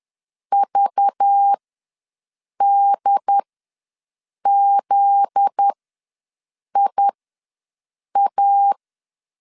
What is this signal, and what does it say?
Morse code, heard as a keyed tone, about 10 wpm, spaced as for 6 wpm, 800 Hz. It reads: VDZIA
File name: LEZIONI / 10 SIGNS / 10D COMA ws 10wpm 4314